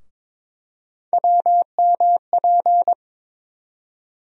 Morse code, heard as a keyed tone, about 22 wpm, 700 Hz. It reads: WMP
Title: USA Call Sign Postfixes - 3 Letters 22wpm